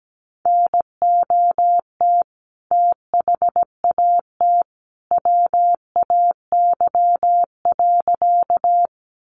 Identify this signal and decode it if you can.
Morse code, heard as a keyed tone, about 17 wpm, 700 Hz. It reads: NOT THAT WAY.
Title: A Sound Of Thunder 17wpm